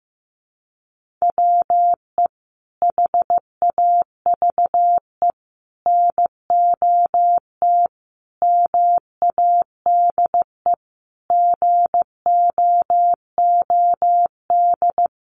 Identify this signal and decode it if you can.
Morse code, heard as a keyed tone, about 15 wpm, 700 Hz. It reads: WE HAVE NOT MADE GOOD
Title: Sentences from Top 1400 Words 15wpm